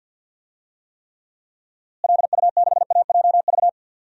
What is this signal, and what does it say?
Morse code, heard as a keyed tone, about 47 wpm, 700 Hz. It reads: LU6AJV
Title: DXCC-100 Call Signs - 6 Characters - Repeated 2x - 50wpm